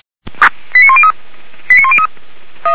Five-tone selective call (ZVEI), heard as quick runs of five tones, tones of about 70 ms. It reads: 89103 89103